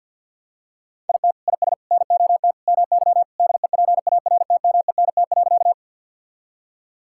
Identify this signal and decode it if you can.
Morse code, heard as a keyed tone, about 50 wpm, 700 Hz. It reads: IT IS NOT MY DEPARTMENT.
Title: Sentences from Top 1000 Words - Review Part 5 - 50wpm